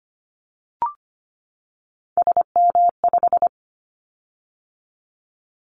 Morse code, heard as a keyed tone, about 25 wpm, 700 Hz. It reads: SM5
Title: Callsign Prefixes for Top 25 DXCC - 25wpm